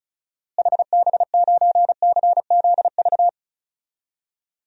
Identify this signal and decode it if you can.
Morse code, heard as a keyed tone, about 36 wpm, 700 Hz. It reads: HB9CZV
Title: DXCC-100 Call Signs - 6 Characters - Repeated 3x - 35wpm